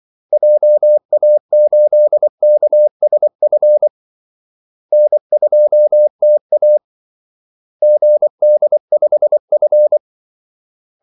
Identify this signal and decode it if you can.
Morse code, heard as a keyed tone, about 24 wpm, 600 Hz. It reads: JA8KSF N2TA GD5F
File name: Callsigns-24wpm